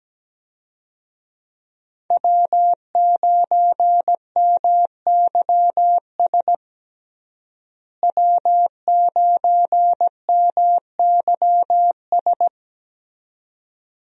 Morse code, heard as a keyed tone, about 17 wpm, 700 Hz. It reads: W9MYS W9MYS
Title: USA Call Signs - 5 Characters - Repeated 2x - 17wpm